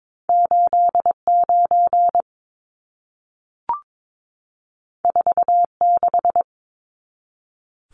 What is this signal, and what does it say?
Morse code, heard as a keyed tone, about 22 wpm, 700 Hz. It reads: 89 46
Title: Numbers - 2-Digit 22wpm